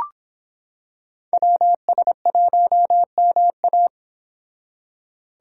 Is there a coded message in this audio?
Yes: Morse code from 1.3 to 3.9 s, heard as a keyed tone, about 26 wpm, 700 Hz. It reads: WS1MA